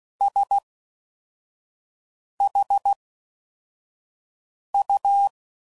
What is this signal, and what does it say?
Morse code, heard as a keyed tone, about 16 wpm, spaced as for 4 wpm, 800 Hz. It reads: SHU